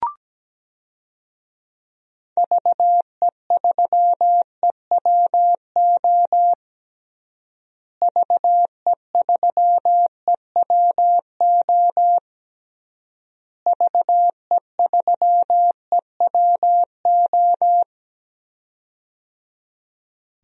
Morse code, heard as a keyed tone, about 17 wpm, 700 Hz. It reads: VE3EWO VE3EWO VE3EWO
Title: DXCC-100 Call Signs - 6 Characters - Repeated 3x - 17wpm